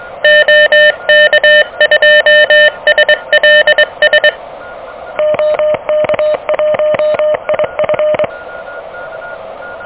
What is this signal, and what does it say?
Morse code, heard as a keyed tone, about 24 wpm, 590 Hz, with noise. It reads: OK2SLS OK1IF
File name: OK QRP 2023 / OK QRP 2023